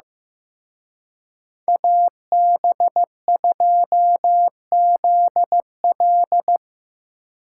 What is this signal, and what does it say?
Morse code, heard as a keyed tone, about 18 wpm, 700 Hz. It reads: AB2ZL
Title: Speed Racing - USA 2x2 Call Signs 15wpm